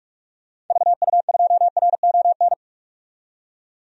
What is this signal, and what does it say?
Morse code, heard as a keyed tone, about 46 wpm, 700 Hz. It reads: VU2RON
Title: DXCC-100 Call Signs - 6 Characters - 45wpm